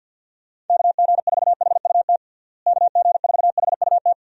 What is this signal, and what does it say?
Morse code, heard as a keyed tone, about 52 wpm, 700 Hz. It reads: KG4HUT KG4HUT
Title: USA Call Signs - 6 Characters - Repeated 3x - 50wpm